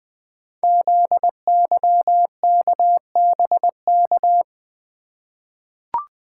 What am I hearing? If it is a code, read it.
Morse code, heard as a keyed tone, about 20 wpm, 700 Hz. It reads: ZYKBK
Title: Code Groups - 5 Letters - 20wpm